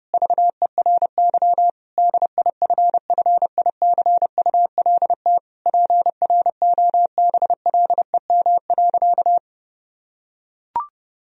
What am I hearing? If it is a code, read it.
Morse code, heard as a keyed tone, about 30 wpm, 700 Hz. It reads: VERY DIFFICULT PROBLEM.
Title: Sentences from Top 1000 Words - Review Part 2 - 30wpm